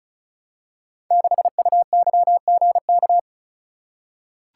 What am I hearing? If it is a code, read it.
Morse code, heard as a keyed tone, about 36 wpm, 700 Hz. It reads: 6UYGK
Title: Code Groups - 5 Letter:Numbers - 35wpm